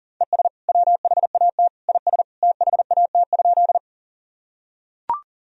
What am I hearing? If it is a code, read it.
Morse code, heard as a keyed tone, about 40 wpm, 700 Hz. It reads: ES WHAT IS THAT?
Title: Sentences from Top 100 Words and Abbreviations 40wpm